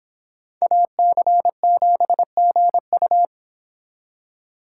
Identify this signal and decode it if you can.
Morse code, heard as a keyed tone, about 26 wpm, 700 Hz. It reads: AC7GU